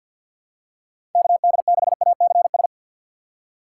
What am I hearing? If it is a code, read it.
Morse code, heard as a keyed tone, about 48 wpm, 690 Hz. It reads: KD6ACS
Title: USA Call Signs - 6x Character Spacing - 50wpm